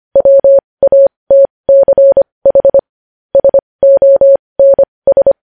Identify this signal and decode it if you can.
Morse code, heard as a keyed tone, about 25 wpm, spaced as for 20 wpm, 550 Hz. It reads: WATCH SONS
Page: News Headlines in Morse Code at 20 WPM